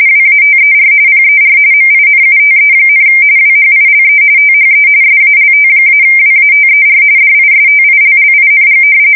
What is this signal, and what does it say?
RTTY, about 45.45 baud. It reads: WATER PER MINUTE, WHILE THE SALT-WATER DISTILLING
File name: fsk170 / battleshipnj